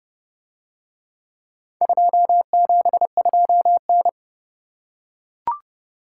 Morse code, heard as a keyed tone, about 29 wpm, 700 Hz. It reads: 272N